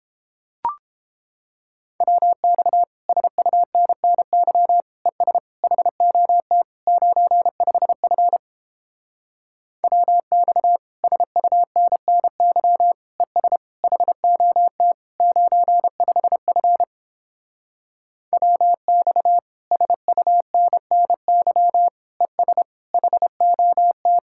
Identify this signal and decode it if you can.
Morse code, heard as a keyed tone, about 31 wpm, 700 Hz. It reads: WX SUNNY ES HOT 95F WX SUNNY ES HOT 95F WX SUNNY ES HOT